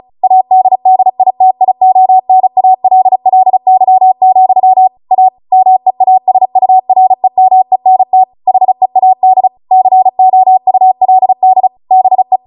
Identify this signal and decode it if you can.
Morse code, heard as a keyed tone, about 36 wpm, 750 Hz. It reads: ADDITIONALLY, A MEASUREMENT HEAD COULD BE